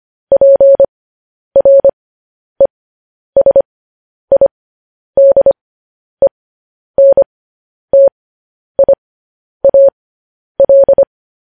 Morse code, heard as a keyed tone, about 25 wpm, spaced as for 10 wpm, 550 Hz. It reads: PRESIDENTIAL
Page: news-headlines-in-morse-code-at-10-wpm